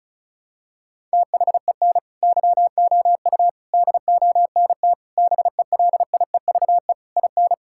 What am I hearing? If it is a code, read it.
Morse code, heard as a keyed tone, about 36 wpm, 700 Hz. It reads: THEN YOU DONT BELIEVE IN
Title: Sentences from Top 1000 Words - Review Part 6 - 35wpm